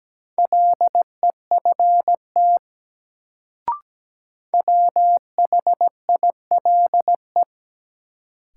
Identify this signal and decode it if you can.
Morse code, heard as a keyed tone, about 17 wpm, 700 Hz. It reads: LEFT WHILE
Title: Sentences from Top 300 Words - New Word Drill - 17wpm